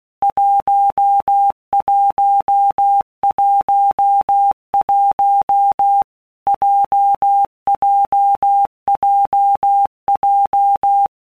Morse code, heard as a keyed tone, about 16 wpm, 800 Hz. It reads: 1111 JJJJ
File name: Basic Letter Groups By Blocks / Groups@15 / JZ17@15